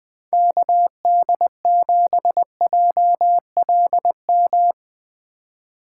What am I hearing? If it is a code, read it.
Morse code, heard as a keyed tone, about 22 wpm, 700 Hz. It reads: KD7JLM
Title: Speed Racing - USA Call Signs 20wpm